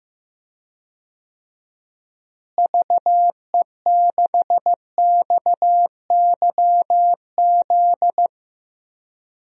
Morse code, heard as a keyed tone, about 15 wpm, 700 Hz. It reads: VE6XYZ
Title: DXCC-100 Call Signs - 6 Characters - Repeated 3x - 15wpm